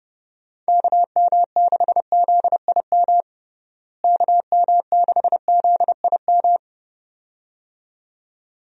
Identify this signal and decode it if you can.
Morse code, heard as a keyed tone, about 30 wpm, 700 Hz. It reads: KM6ZIM KM6ZIM
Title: USA Call Signs - 6 Characters - Repeated 2x - 30wpm